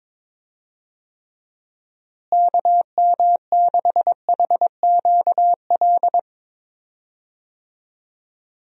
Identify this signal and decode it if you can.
Morse code, heard as a keyed tone, about 22 wpm, 700 Hz. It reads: KM6HQL
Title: USA Call Signs - 6 Characters - Repeated 3x - 22wpm